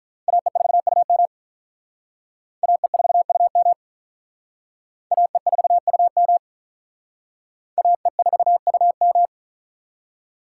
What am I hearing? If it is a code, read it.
Morse code, heard as a keyed tone, about 44 wpm, 700 Hz. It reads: AE4UM AE4UM AE4UM AE4UM
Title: Speed Racing - USA 2x2 Call Signs 35wpm